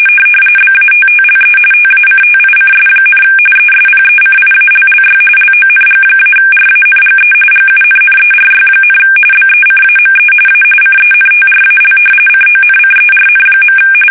RTTY, about 45.45 baud. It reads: BATTLE TACTICS, ESPECIALLY LONG-RANGE GUNNERY, BECAME A CENTRAL CONCERN.
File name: fsk850 / usnavy3